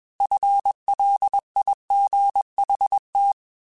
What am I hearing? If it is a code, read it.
Morse code, heard as a keyed tone, about 21 wpm, 800 Hz. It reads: FLIGHT
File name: Alphabetical / Alphabetical mp3s / Fs @20